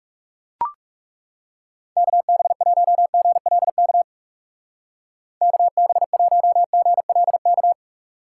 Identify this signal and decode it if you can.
Morse code, heard as a keyed tone, about 37 wpm, 700 Hz. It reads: KB1GLK KB1GLK
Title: Speed Racing - USA 2x3 Call Signs 30wpm